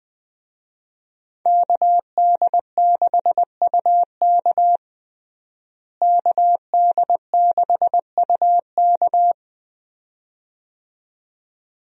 Morse code, heard as a keyed tone, about 20 wpm, 700 Hz. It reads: KD6UK KD6UK
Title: USA Call Signs - 5 Characters - Repeated 2x - 20wpm